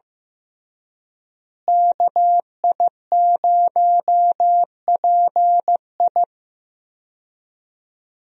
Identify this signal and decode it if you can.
Morse code, heard as a keyed tone, about 15 wpm, 700 Hz. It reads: KI0PI
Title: USA Call Signs - 5 Characters - Repeated 2x - 15wpm